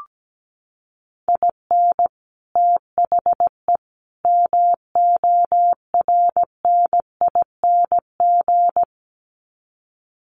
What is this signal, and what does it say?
Morse code, heard as a keyed tone, about 17 wpm, 700 Hz. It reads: IN THE MORNING
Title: Sets of 3 Words 17wpm